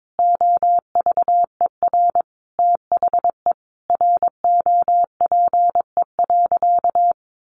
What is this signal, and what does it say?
Morse code, heard as a keyed tone, about 22 wpm, 700 Hz. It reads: OVER THE ROPE.
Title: Easy Phrases 22wpm